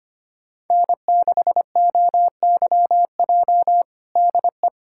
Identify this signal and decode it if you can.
Morse code, heard as a keyed tone, about 25 wpm, 700 Hz. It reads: N6OYJ DE
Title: USA Callsign DE Callsign - 5 Characters - 25wpm